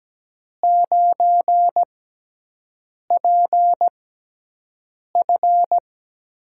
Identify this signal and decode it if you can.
Morse code, heard as a keyed tone, about 17 wpm, spaced as for 6 wpm, 700 Hz. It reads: 9PF
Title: USA Call Signs - 5 Characters - 6x Character Spacing - 17wpm